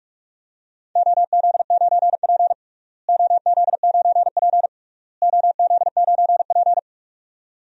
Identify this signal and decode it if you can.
Morse code, heard as a keyed tone, about 42 wpm, 700 Hz. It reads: OZ9P OZ9P OZ9P
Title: DXCC-100 Call Signs - 4 Characters - Repeated 3x - 45wpm